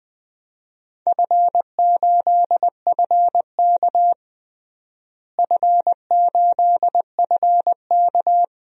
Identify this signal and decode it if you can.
Morse code, heard as a keyed tone, about 20 wpm, 700 Hz. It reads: F8FK F8FK
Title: DXCC-100 Call Signs - 4 Characters - Repeated 2x - 20wpm